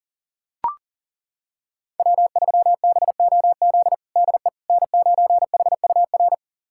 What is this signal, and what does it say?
Morse code, heard as a keyed tone, about 40 wpm, 700 Hz. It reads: W3BOZ DE N9HUR
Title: USA Callsign DE Callsign - 5 Characters - 40wpm